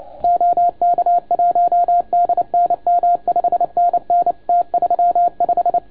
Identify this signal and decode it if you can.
Morse code, heard as a keyed tone, about 30 wpm, 700 Hz. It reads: OK1DNM 5NN T35
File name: Archiv / CUC287 / OK1FGD CUC 287